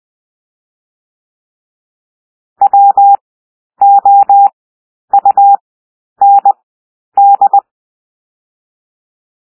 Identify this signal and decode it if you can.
Morse code, heard as a keyed tone, about 20 wpm, 800 Hz. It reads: W O U N D
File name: cwmp3 / W1 SOUND WORDS 20 wpm